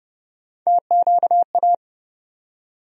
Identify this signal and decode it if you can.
Morse code, heard as a keyed tone, about 32 wpm, 700 Hz. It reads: TQA
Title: USA Call Sign Postfixes - 3 Letters 30wpm